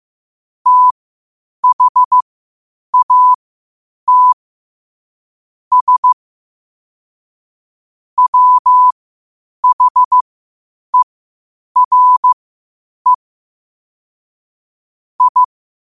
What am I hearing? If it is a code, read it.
Morse code, heard as a keyed tone, about 15 wpm, spaced as for 8 wpm, 1000 Hz. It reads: THAT S WHERE I
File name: qotdfiles / 8wpm